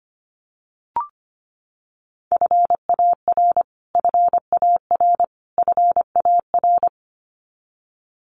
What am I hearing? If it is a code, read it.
Morse code, heard as a keyed tone, about 25 wpm, 700 Hz. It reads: FAR FAR FAR
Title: Top 500 Words Repeated 25wpm